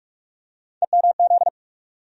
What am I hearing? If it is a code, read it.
Morse code, heard as a keyed tone, about 46 wpm, 700 Hz. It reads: EMZ
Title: USA Call Sign Postfixes - 3 Letters 45wpm